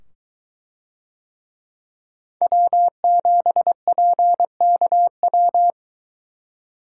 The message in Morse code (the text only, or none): W7PKW